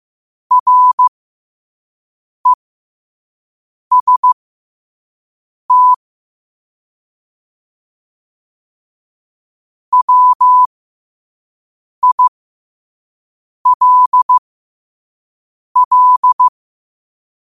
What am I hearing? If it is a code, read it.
Morse code, heard as a keyed tone, about 15 wpm, spaced as for 5 wpm, 1000 Hz. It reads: REST WILL
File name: qotdfiles / 5wpm